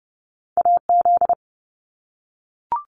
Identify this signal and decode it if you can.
Morse code, heard as a keyed tone, about 31 wpm, 700 Hz. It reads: AZ
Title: US States 30wpm